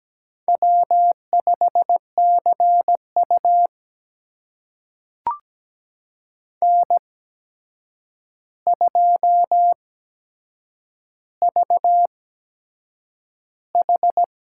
Morse code, heard as a keyed tone, about 17 wpm, spaced as for 7 wpm, 700 Hz. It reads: W5CU N2VH
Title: USA Call Signs - 4 Characters - 8x Character Spacing - 17wpm